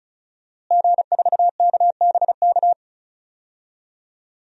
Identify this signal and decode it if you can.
Morse code, heard as a keyed tone, about 36 wpm, 700 Hz. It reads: G4KBK
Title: DXCC-100 Call Signs - 5 Characters - Repeated 2x - 35wpm